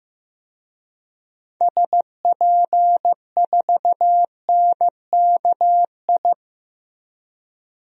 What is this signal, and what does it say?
Morse code, heard as a keyed tone, about 15 wpm, 700 Hz. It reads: SP4NKI
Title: DXCC-100 Call Signs - 6 Characters - Repeated 2x - 15wpm